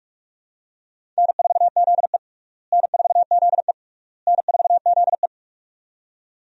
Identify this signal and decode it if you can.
Morse code, heard as a keyed tone, about 41 wpm, 700 Hz. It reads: N4ZE N4ZE N4ZE
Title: USA Call Signs - 4 Characters - Repeated 3x - 45wpm